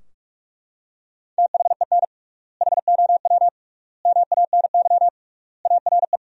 Morse code, heard as a keyed tone, about 41 wpm, 700 Hz. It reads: THEN HOW MANY ARE
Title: 3x Word-Spacing - Sentences from Top 100 Words - 45wpm